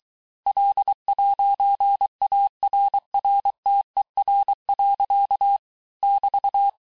Morse code, heard as a keyed tone, about 23 wpm, 800 Hz. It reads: L'ARRTER. <BT>
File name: cours-du-mercredi / audios / COURS-855--texte-clair-vitesse-22-mpm